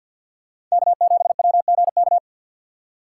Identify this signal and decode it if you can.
Morse code, heard as a keyed tone, about 50 wpm, 700 Hz. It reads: K7WGK
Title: USA Call Signs - 6x Character Spacing - 50wpm